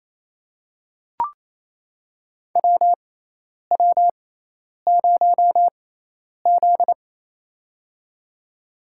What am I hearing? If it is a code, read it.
Morse code, heard as a keyed tone, about 27 wpm, spaced as for 10 wpm, 700 Hz. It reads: WW0Z
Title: USA Call Signs - 4 Characters - 6x Character Spacing - 28wpm